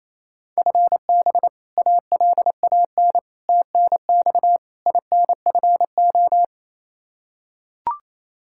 Morse code, heard as a keyed tone, about 28 wpm, 700 Hz. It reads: FB ALAN TNX INFO
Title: Practice QSO - Exchange 3 - More Info - 28wpm